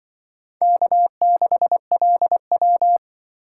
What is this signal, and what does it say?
Morse code, heard as a keyed tone, about 24 wpm, 700 Hz. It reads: K6LW
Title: Speed Racing - USA 1x2 Call Signs 20wpm